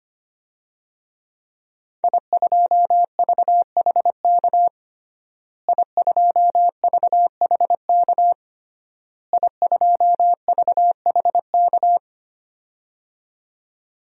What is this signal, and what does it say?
Morse code, heard as a keyed tone, about 25 wpm, 700 Hz. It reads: I2VHK I2VHK I2VHK
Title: DXCC-100 Call Signs - 5 Characters - Repeated 3x - 25wpm